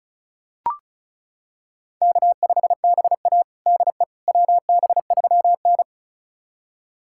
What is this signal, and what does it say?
Morse code, heard as a keyed tone, about 34 wpm, 700 Hz. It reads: K5BA DE WB3N
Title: USA Callsign DE Callsign - 4 Characters - 35wpm